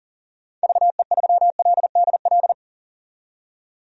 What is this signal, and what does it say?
Morse code, heard as a keyed tone, about 37 wpm, 700 Hz. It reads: VE3LDL
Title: DXCC-100 Call Signs - 6 Characters - 40wpm